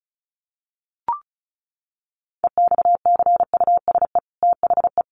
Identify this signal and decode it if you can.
Morse code, heard as a keyed tone, about 34 wpm, 700 Hz. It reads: EXCUSE THE
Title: Sentences from Top 800 Words 35wpm